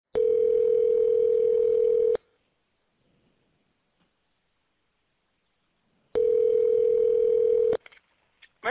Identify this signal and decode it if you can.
Ringback tone (North America), 440 + 480 Hz.